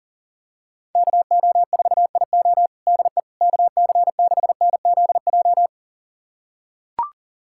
Morse code, heard as a keyed tone, about 40 wpm, 700 Hz. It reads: KO4IO DE KC6NZJ
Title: USA Callsign DE Callsign 40wpm